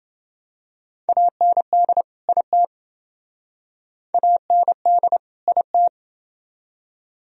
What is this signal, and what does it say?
Morse code, heard as a keyed tone, about 28 wpm, 700 Hz. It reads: AND IT AND IT